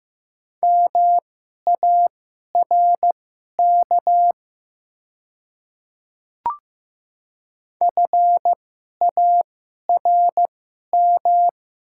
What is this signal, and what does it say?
Morse code, heard as a keyed tone, about 15 wpm, spaced as for 11 wpm, 700 Hz. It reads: MARK FARM
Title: 2x Character Spacing - Top 500 Words - 15wpm